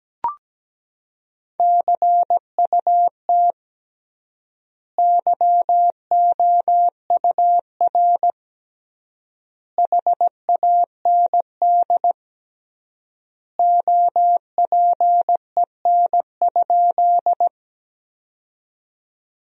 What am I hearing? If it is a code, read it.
Morse code, heard as a keyed tone, about 17 wpm, 700 Hz. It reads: CUT YOUR HAND OPEN?
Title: 3x Word-Spacing - Sentences from Top 300 Words - 17wpm